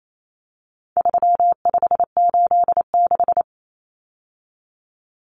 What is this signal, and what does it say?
Morse code, heard as a keyed tone, about 28 wpm, 700 Hz. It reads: 3586
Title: Numbers - 4-Digit 28wpm